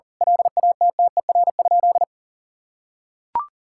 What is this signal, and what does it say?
Morse code, heard as a keyed tone, about 40 wpm, 700 Hz. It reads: LATTER?
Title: Sentences from Top 1600 Words 40wpm